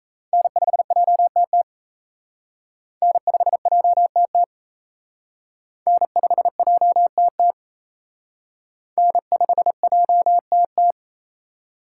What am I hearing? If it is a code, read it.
Morse code, heard as a keyed tone, about 34 wpm, 700 Hz. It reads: N5JTT N5JTT N5JTT N5JTT
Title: Speed Racing - USA 1x3 Call Signs 28wpm